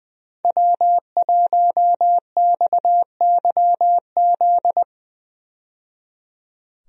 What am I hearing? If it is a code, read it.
Morse code, heard as a keyed tone, about 20 wpm, 700 Hz. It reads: W1XYZ
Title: USA Call Signs - 5 Characters - 20wpm